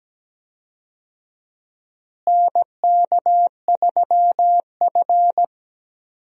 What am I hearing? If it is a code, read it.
Morse code, heard as a keyed tone, about 17 wpm, 700 Hz. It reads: NK3F
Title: USA Call Signs - 4 Characters - Repeated 3x - 17wpm